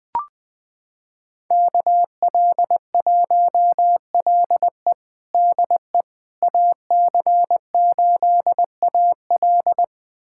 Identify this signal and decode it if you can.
Morse code, heard as a keyed tone, about 21 wpm, 700 Hz. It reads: KL1LE DE AC8AL
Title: USA Callsign DE Callsign - 5 Characters - 20wpm